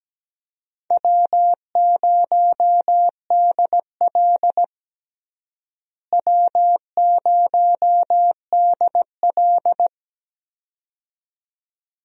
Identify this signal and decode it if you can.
Morse code, heard as a keyed tone, about 17 wpm, 700 Hz. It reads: W0DL W0DL
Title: USA Call Signs - 4 Characters - Repeated 2x - 17wpm